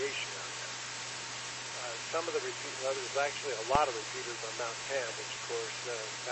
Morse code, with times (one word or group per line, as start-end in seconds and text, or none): none